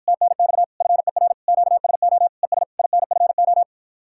Morse code, heard as a keyed tone, about 54 wpm, 700 Hz. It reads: TNX FER QSO ES INFO